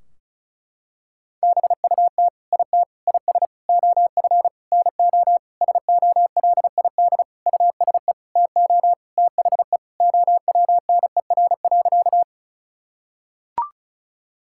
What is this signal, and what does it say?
Morse code, heard as a keyed tone, about 36 wpm, 700 Hz. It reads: BUT IT IS OF NO SOLID USE TO THE OWNER.
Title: Sentences from Top 1600 Words 35wpm